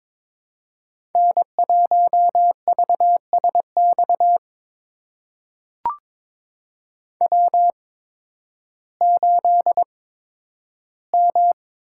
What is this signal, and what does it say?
Morse code, heard as a keyed tone, about 22 wpm, spaced as for 12 wpm, 700 Hz. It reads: N1VSX W8M
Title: USA Call Signs - 5 Characters - 8x Character Spacing - 22wpm